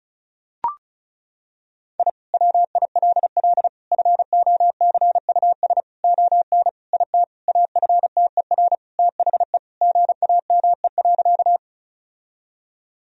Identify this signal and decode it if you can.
Morse code, heard as a keyed tone, about 36 wpm, 700 Hz. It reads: I WILL FOCUS ON IT AFTER THE GAME.
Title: Sentences from Top 1000 Words - Review Part 4 - 35wpm